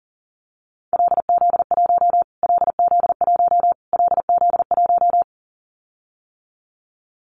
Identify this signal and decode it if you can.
Morse code, heard as a keyed tone, about 40 wpm, 700 Hz. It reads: LZ1 LZ1 LZ1
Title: Callsign Prefixes for Top 25 DXCC - Repeated 3x - 40wpm